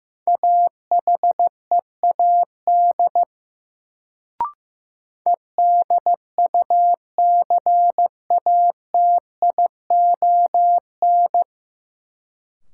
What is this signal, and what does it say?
Morse code, heard as a keyed tone, about 15 wpm, 700 Hz. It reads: AHEAD EDUCATION
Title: Sentences from Top 1000 Words - Review Part 6 - New Word Drill - 15wpm